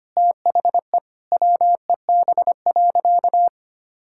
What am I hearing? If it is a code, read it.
Morse code, heard as a keyed tone, about 25 wpm, 700 Hz. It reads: THE WEB.
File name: practice / 25wpm / All-Age Appropriate Jokes - 25wpm